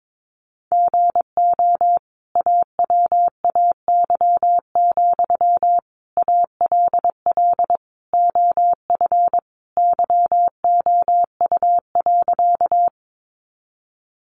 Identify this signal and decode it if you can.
Morse code, heard as a keyed tone, about 22 wpm, 700 Hz. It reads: GO AWAY, ALL OF YOU.
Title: Sentences from Top 600 Words 22wpm